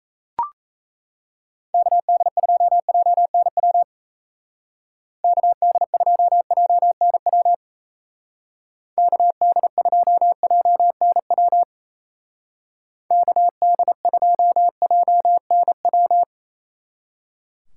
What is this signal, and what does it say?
Morse code, heard as a keyed tone, about 35 wpm, 700 Hz. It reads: KD2JNW KD2JNW KD2JNW KD2JNW
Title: Speed Racing - USA Call Signs 28wpm